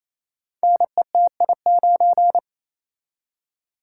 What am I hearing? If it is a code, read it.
Morse code, heard as a keyed tone, about 27 wpm, 700 Hz. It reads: NETI9
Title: Code Groups - 5 Letter:Numbers - 28wpm